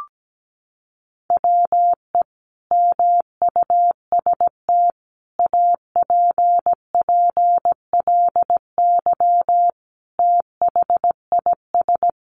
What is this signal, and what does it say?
Morse code, heard as a keyed tone, about 17 wpm, 700 Hz. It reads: WE MUST APPLY THIS
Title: Sentences from Top 1400 Words 17wpm